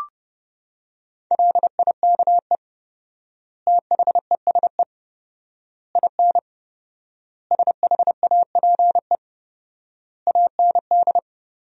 Morse code, heard as a keyed tone, about 32 wpm, 700 Hz. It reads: LIKE THESE IN SHAPE AND
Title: 4x Word-Spacing - Sentences from Top 500 Words - 30wpm